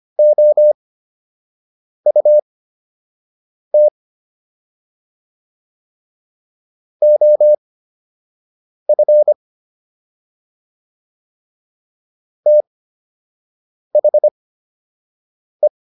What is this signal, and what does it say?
Morse code, heard as a keyed tone, about 25 wpm, spaced as for 6 wpm, 600 Hz. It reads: OUT OF THE